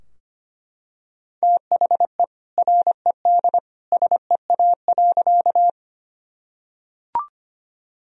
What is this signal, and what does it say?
Morse code, heard as a keyed tone, about 25 wpm, 700 Hz. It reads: THE RED SEA.
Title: Sentences from Top 1000 Words - Review Part 2 - 25wpm